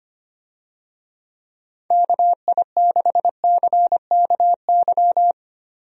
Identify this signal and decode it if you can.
Morse code, heard as a keyed tone, about 25 wpm, 700 Hz. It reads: KI6CKY
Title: USA Call Signs - 6 Characters - Repeated 3x - 25wpm